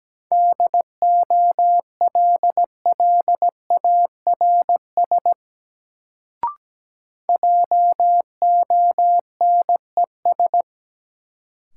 Morse code, heard as a keyed tone, about 17 wpm, 700 Hz. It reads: DOLLARS JONES
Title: Sets of 5 Words - New Word Drill - 17wpm